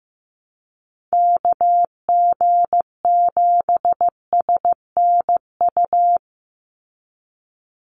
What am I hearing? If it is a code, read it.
Morse code, heard as a keyed tone, about 15 wpm, 700 Hz. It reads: KG7SNU
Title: USA Call Signs 15wpm